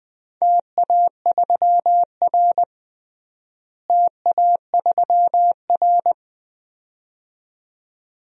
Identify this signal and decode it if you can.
Morse code, heard as a keyed tone, about 20 wpm, 700 Hz. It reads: TA3R TA3R
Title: DXCC-100 Call Signs - 4 Characters - Repeated 2x - 20wpm